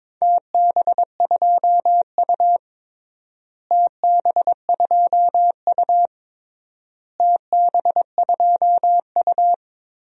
Morse code, heard as a keyed tone, about 22 wpm, 700 Hz. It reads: TB2U TB2U TB2U
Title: DXCC-100 Call Signs - 4 Characters - Repeated 3x - 22wpm